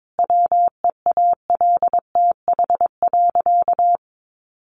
Morse code, heard as a keyed tone, about 22 wpm, 700 Hz. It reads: WEALTH.
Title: Sentences from Top 1400 Words 22wpm